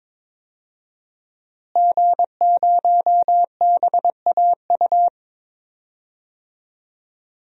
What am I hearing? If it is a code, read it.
Morse code, heard as a keyed tone, about 22 wpm, 700 Hz. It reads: G0BAU
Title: DXCC-100 Call Signs - 5 Characters - Repeated 3x - 22wpm